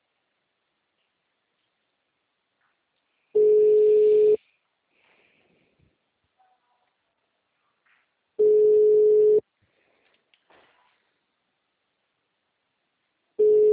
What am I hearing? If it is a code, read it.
Ringback tone (Europe, 425 Hz).